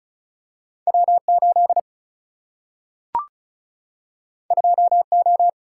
Morse code, heard as a keyed tone, about 36 wpm, 700 Hz. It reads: W8 2O
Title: Single-Letter Single-Number 35wpm